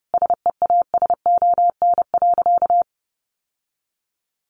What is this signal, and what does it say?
Morse code, heard as a keyed tone, about 30 wpm, 700 Hz. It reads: SEASON.
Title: Sentences from Top 1400 Words 30wpm